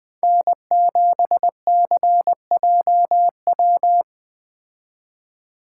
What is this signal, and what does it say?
Morse code, heard as a keyed tone, about 20 wpm, 700 Hz. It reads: N7CJW
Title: Speed Racing - USA 1x3 Call Signs 17wpm